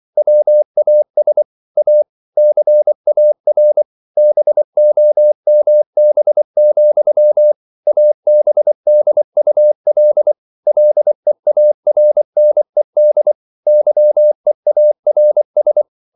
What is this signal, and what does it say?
Morse code, heard as a keyed tone, about 24 wpm, 600 Hz. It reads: WAS A CAR BOMB, ABDUL LEARNED YEARS